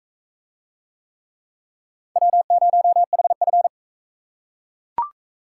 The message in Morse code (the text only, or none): W0HF